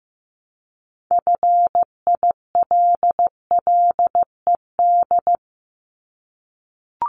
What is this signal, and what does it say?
Morse code, heard as a keyed tone, about 15 wpm, 700 Hz. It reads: FILLED
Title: Multi-letter Phonemes Repeated 3x 15wpm